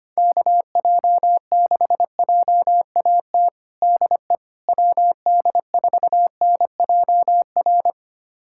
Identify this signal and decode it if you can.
Morse code, heard as a keyed tone, about 25 wpm, 700 Hz. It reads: KJ6JAT DE WD4NJR
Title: USA Callsign DE Callsign - 6 Characters - 25wpm